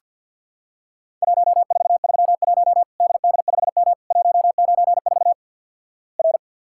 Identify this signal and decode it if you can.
Morse code, heard as a keyed tone, about 50 wpm, 700 Hz. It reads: 1431 DD5M 194 R
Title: Worked All Europe DX Contest - QTC Exchange with Repeats - 50wpm